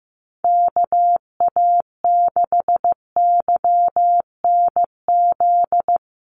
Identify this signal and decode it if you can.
Morse code, heard as a keyed tone, about 15 wpm, 700 Hz. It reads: KA6YNZ
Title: USA Call Signs Repeated 2x 15wpm